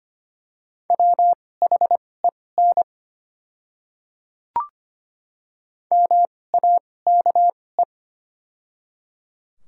Morse code, heard as a keyed tone, about 25 wpm, spaced as for 18 wpm, 700 Hz. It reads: WHEN MAKE
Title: 2x Character Spacing - Top 100 Words - 25wpm